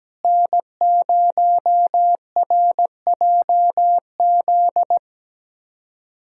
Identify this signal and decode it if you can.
Morse code, heard as a keyed tone, about 17 wpm, 700 Hz. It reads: N0RJZ
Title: USA Callsign DE Callsign - 5 Characters - 17wpm